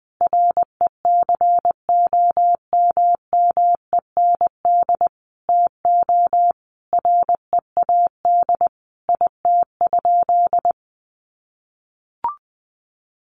Morse code, heard as a keyed tone, about 20 wpm, 700 Hz. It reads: RECOMMEND TO READ IT?
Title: Sentences from Top 1200 Words 20wpm